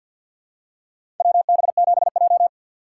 Morse code, heard as a keyed tone, about 51 wpm, 700 Hz. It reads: WB7J
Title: USA Call Signs - 4 Characters - 3x Character Spacing - 50wpm